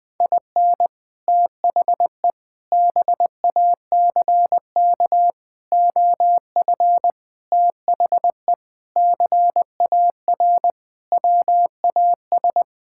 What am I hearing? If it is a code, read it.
Morse code, heard as a keyed tone, about 20 wpm, 700 Hz. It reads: IN THE BACK OF THE CAR WAS
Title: Sentences with 4-Word Phrases - Spoken Beginning - 20wpm